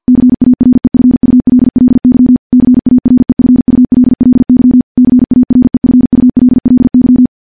Morse code, heard as a keyed tone, about 50 wpm, 250 Hz. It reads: CAREFULLY CAREFULLY CAREFULLY
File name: morse / morse-training-50WPM